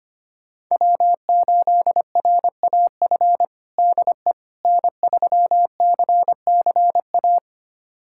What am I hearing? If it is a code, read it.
Morse code, heard as a keyed tone, about 25 wpm, 700 Hz. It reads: W8RAF DE N3CCA
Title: USA Callsign DE Callsign - 5 Characters - 25wpm